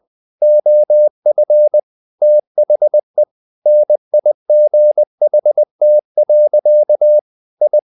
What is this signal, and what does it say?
Morse code, heard as a keyed tone, about 20 wpm, 600 Hz. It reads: OF THE NIGHT. I
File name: the old man and the sea part 2 20wpm